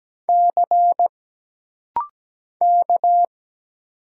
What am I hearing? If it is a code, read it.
Morse code, heard as a keyed tone, about 17 wpm, 700 Hz. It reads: C K
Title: Single Letter-Number - Rapid-Fire 17wpm